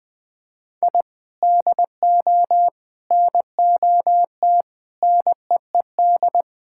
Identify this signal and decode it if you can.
Morse code, heard as a keyed tone, about 20 wpm, 700 Hz. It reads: I DO NOT NEED
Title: Sentences with 5-Word Phrases - 20wpm